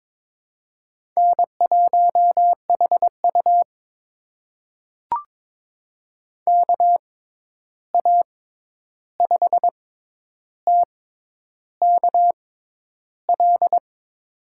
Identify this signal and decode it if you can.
Morse code, heard as a keyed tone, about 22 wpm, spaced as for 10 wpm, 700 Hz. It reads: N1HU KA5TKL
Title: USA Call Signs - 6x Character Spacing - 22wpm